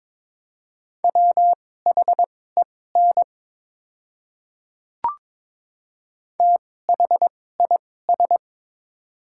Morse code, heard as a keyed tone, about 22 wpm, spaced as for 16 wpm, 700 Hz. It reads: WHEN THIS
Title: 2x Character Spacing - Top 200 Words - 22wpm